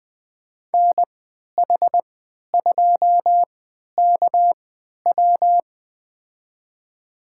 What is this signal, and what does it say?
Morse code, heard as a keyed tone, about 20 wpm, spaced as for 11 wpm, 700 Hz. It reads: NH2KW
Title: USA Call Signs - 3x Character Spacing - 20wpm